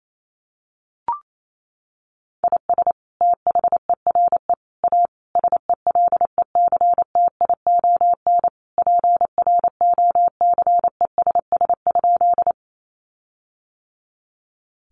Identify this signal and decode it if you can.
Morse code, heard as a keyed tone, about 28 wpm, 700 Hz. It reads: IS THERE A SELECTION PROCESS?
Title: Sentences from Top 1400 Words 28wpm